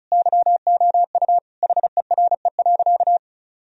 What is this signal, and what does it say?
Morse code, heard as a keyed tone, about 36 wpm, 700 Hz. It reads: YOU HERE.
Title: Sentences from Top 1000 Words - Review Part 2 - 35wpm